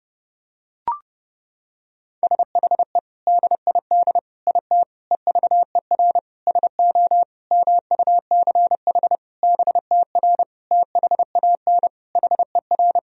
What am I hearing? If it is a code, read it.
Morse code, heard as a keyed tone, about 30 wpm, 700 Hz. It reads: SHE DID IT EVER SO MUCH BTR THAN HER
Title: Sentences from Top 400 Words and Abbreviations 30wpm